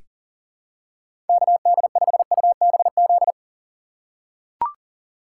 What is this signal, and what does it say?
Morse code, heard as a keyed tone, about 38 wpm, 700 Hz. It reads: KD5UBZ
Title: USA Call Signs - 6 Characters - 3x Character Spacing - 40wpm